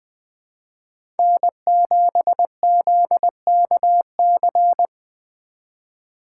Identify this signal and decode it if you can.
Morse code, heard as a keyed tone, about 20 wpm, 700 Hz. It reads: N7ZKC
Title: USA Call Signs - 3x Character Spacing - 20wpm